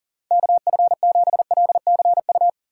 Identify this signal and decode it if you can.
Morse code, heard as a keyed tone, about 50 wpm, 700 Hz. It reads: KF7LCU
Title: Speed Racing - USA Call Signs 40wpm